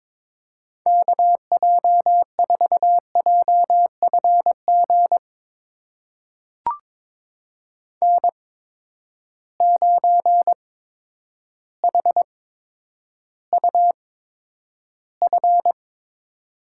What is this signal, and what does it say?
Morse code, heard as a keyed tone, about 22 wpm, spaced as for 10 wpm, 700 Hz. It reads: KJ4JFG N9HUF
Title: USA Call Signs - 8x Character Spacing - 22wpm